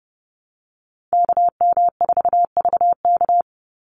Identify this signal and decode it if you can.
Morse code, heard as a keyed tone, about 30 wpm, 700 Hz. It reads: KM4VK
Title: USA Call Signs 30wpm